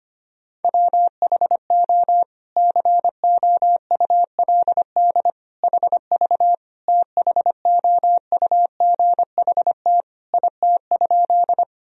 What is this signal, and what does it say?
Morse code, heard as a keyed tone, about 25 wpm, 700 Hz. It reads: WHO COULD HV THOUGHT IT?
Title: Sentences from Top 300 Words and Abbreviations 25wpm